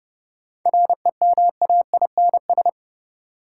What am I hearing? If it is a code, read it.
Morse code, heard as a keyed tone, about 30 wpm, 700 Hz. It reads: REMAINS
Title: Sets of 5 Words Encore - New Word Drill - 30wpm